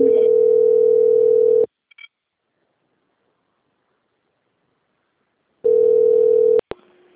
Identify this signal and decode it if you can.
Ringback tone (North America), 440 + 480 Hz.